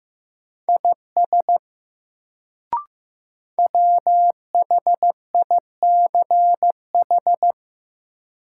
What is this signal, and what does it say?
Morse code, heard as a keyed tone, about 15 wpm, 700 Hz. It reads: IS WHICH
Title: Sentences from Top 100 Words - New Word Drill - 15wpm